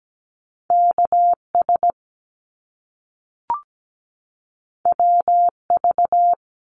Morse code, heard as a keyed tone, about 17 wpm, 700 Hz. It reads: KS WV
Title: US States 17wpm